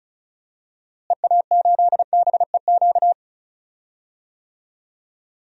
Morse code, heard as a keyed tone, about 34 wpm, 700 Hz. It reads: EA8BEQ
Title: DXCC-100 Call Signs - 6 Characters - Repeated 2x - 35wpm